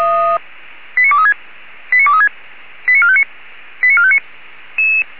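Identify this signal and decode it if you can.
Five-tone selective call (ZVEI), heard as quick runs of five tones, tones of about 70 ms. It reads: 89236 89236 89459 89459
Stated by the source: ZVEI five-tone selective call received by SDR.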